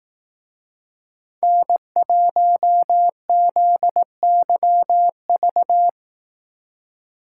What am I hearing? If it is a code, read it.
Morse code, heard as a keyed tone, about 18 wpm, 700 Hz. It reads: N1ZYV
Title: Speed Racing - USA Call Signs 15wpm